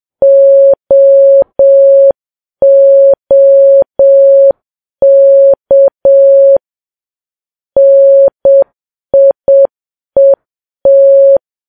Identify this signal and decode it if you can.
Morse code, heard as a keyed tone, about 7 wpm, 550 Hz. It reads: OOK NIET
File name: Files Web tls / Morse on9fv / morse7kps